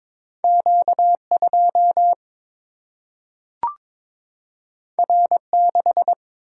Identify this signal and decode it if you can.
Morse code, heard as a keyed tone, about 22 wpm, 700 Hz. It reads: Q2 R6
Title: Single-Letter Single-Number 22wpm